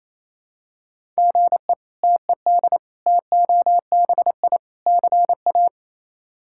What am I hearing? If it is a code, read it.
Morse code, heard as a keyed tone, about 28 wpm, 700 Hz. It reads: GE TED TOBI CA
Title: K1USN SST QSO Practice - 28wpm